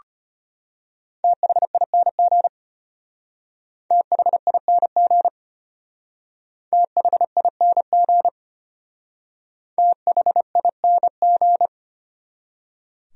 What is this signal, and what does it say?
Morse code, heard as a keyed tone, about 32 wpm, 700 Hz. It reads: THING THING THING THING